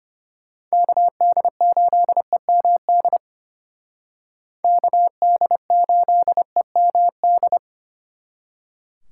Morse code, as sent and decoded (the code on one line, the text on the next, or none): -.- -.. ---.. -- -.. / -.- -.. ---.. -- -..
KD8EMD KD8EMD